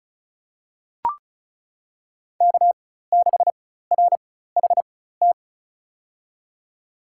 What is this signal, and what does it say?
Morse code, heard as a keyed tone, about 36 wpm, spaced as for 16 wpm, 700 Hz. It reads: K6RHT
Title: USA Call Signs - 5 Characters - 4x Character Spacing - 35wpm